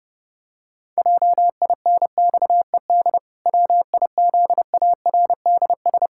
Morse code, heard as a keyed tone, about 30 wpm, 700 Hz. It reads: JINXED WIZARDS
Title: Pangram Challenge v2 - 30wpm